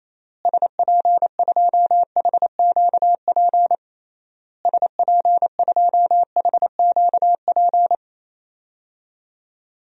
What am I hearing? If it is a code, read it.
Morse code, heard as a keyed tone, about 28 wpm, 700 Hz. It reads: SP2HQP SP2HQP
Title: DXCC-100 Call Signs - 6 Characters - Repeated 3x - 28wpm